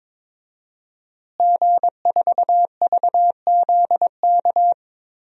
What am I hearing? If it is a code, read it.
Morse code, heard as a keyed tone, about 22 wpm, 700 Hz. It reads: G4VZK
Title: DXCC-100 Call Signs - 5 Characters - Repeated 2x - 22wpm